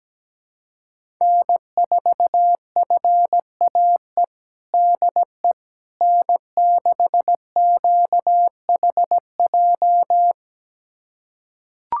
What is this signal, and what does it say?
Morse code, heard as a keyed tone, about 17 wpm, 700 Hz. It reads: N4FAE DE N6QHJ
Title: USA Callsign DE Callsign - 5 Characters - 17wpm